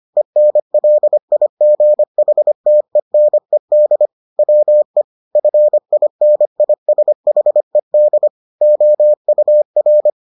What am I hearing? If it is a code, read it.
Morse code, heard as a keyed tone, about 25 wpm, 600 Hz. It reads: ENLIGHTENED WE FINISHED OUR